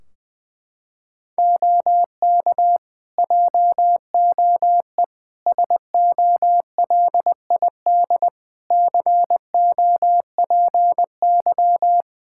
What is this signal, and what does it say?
Morse code, heard as a keyed tone, about 20 wpm, 700 Hz. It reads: OK JOE SOLID COPY
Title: Practice QSO - Exchange 3 - More Info - 20wpm